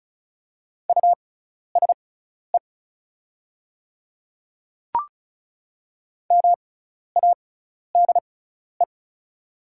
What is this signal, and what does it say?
Morse code, heard as a keyed tone, about 36 wpm, spaced as for 12 wpm, 700 Hz. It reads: USE MADE
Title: 6x Character Spacing - Top 200 Words - 35wpm